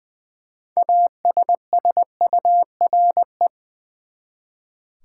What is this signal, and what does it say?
Morse code, heard as a keyed tone, about 20 wpm, 700 Hz. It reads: ASSURE
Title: Six Letter Words - 20wpm